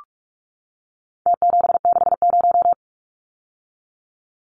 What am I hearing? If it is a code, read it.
Morse code, heard as a keyed tone, about 44 wpm, 700 Hz. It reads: T760
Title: Numbers - Contest Style - 4-Digit 45wpm